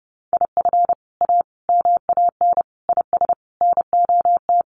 Morse code, heard as a keyed tone, about 30 wpm, 700 Hz. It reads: IF A MAN IS NOT
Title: Sentences from Top 1400 Words 30wpm